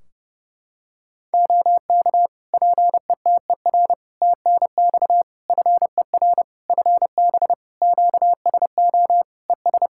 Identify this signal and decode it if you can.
Morse code, heard as a keyed tone, about 30 wpm, 700 Hz. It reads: OK PETER TNX FER FB QSO ES
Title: K1USN SST QSO Practice 30wpm